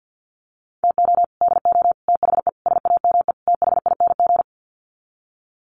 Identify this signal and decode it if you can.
Morse code, heard as a keyed tone, about 50 wpm, 700 Hz. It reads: TO DO THE SAME THING
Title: Sets of 5 Words Encore 50wpm